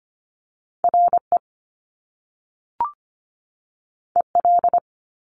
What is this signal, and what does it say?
Morse code, heard as a keyed tone, about 25 wpm, 700 Hz. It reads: R E E L
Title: Top Two Letter Combinations in Words 25wpm